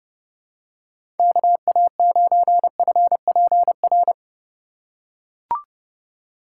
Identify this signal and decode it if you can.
Morse code, heard as a keyed tone, about 31 wpm, 700 Hz. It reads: KA9FPR
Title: USA Call Signs - 6 Characters - 3x Character Spacing - 30wpm